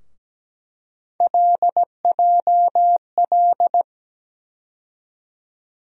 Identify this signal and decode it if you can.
Morse code, heard as a keyed tone, about 17 wpm, 700 Hz. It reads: LJL
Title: USA Call Sign Postfixes - 3 Letters 17wpm